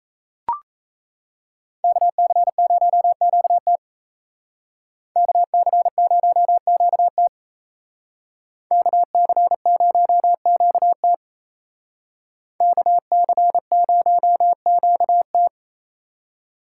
Morse code, heard as a keyed tone, about 34 wpm, 700 Hz. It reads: KC0QT KC0QT KC0QT KC0QT
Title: Speed Racing - USA Call Signs 28wpm